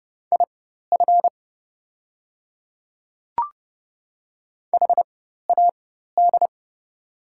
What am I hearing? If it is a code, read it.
Morse code, heard as a keyed tone, about 30 wpm, spaced as for 14 wpm, 700 Hz. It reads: IF HAD
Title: 4x Character Spacing - Top 100 Words - 30wpm